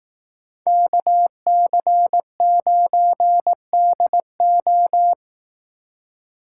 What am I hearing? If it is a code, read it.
Morse code, heard as a keyed tone, about 18 wpm, 700 Hz. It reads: KC9DO
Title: Speed Racing - USA 2x2 Call Signs 15wpm